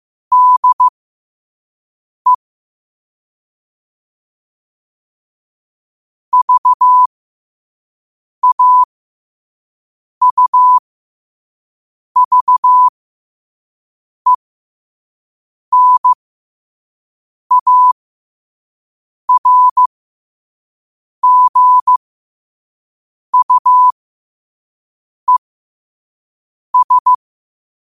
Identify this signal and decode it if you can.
Morse code, heard as a keyed tone, about 15 wpm, spaced as for 5 wpm, 1000 Hz. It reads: DE VAUVENARGUES
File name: 5wpm